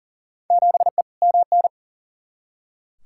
Morse code, heard as a keyed tone, about 40 wpm, 700 Hz. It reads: 7E MN
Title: Field Day Exchanges Encore - 40wpm